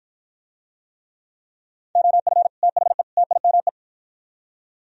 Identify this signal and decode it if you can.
Morse code, heard as a keyed tone, about 52 wpm, 700 Hz. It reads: OF THE TIME